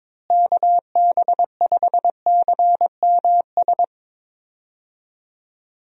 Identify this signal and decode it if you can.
Morse code, heard as a keyed tone, about 22 wpm, 700 Hz. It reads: KB5CMS
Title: USA Call Signs - 6 Characters - Repeated 3x - 22wpm